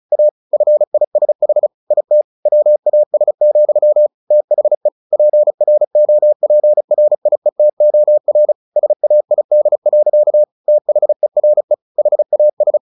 Morse code, heard as a keyed tone, about 35 wpm, 600 Hz. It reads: A FISH IT WAS, THE PROPRIETOR SAID. THERE HAS
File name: the old man and the sea 2 35wpm